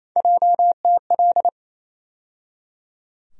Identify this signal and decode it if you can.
Morse code, heard as a keyed tone, about 28 wpm, 700 Hz. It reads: JTL
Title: USA Call Sign Postfixes - 3 Letters 28wpm